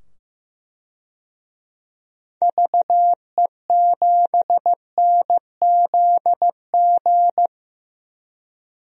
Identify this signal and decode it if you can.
Morse code, heard as a keyed tone, about 16 wpm, 700 Hz. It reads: VE7NZG
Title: DXCC-100 Call Signs - 6 Characters - 15wpm